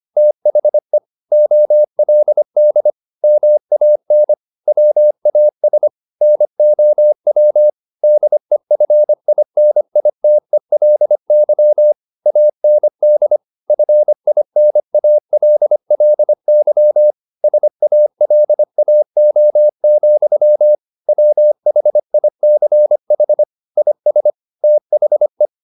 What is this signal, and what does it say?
Morse code, heard as a keyed tone, about 25 wpm, 600 Hz. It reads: THE OLD MAN WAS NOW DEFINITELY AND FINALLY SALAO, WHICH IS THE